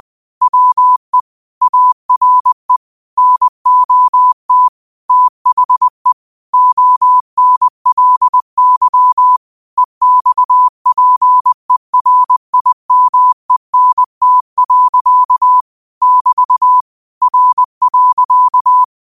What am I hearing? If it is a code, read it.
Morse code, heard as a keyed tone, about 20 wpm, 1000 Hz. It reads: WE ARE NOT THE ONLY EXPERIMENT. <BT> R.
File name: qotdfiles / 20wpm